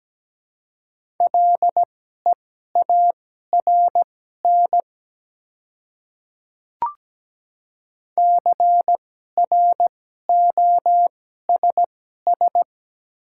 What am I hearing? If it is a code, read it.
Morse code, heard as a keyed tone, about 17 wpm, spaced as for 12 wpm, 700 Hz. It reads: LEARN CROSS
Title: 2x Character Spacing - Top 300 Words - 17wpm